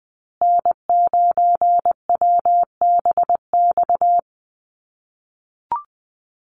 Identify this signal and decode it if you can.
Morse code, heard as a keyed tone, about 20 wpm, 700 Hz. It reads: N9WBX
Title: USA Call Signs Repeated 2x 20wpm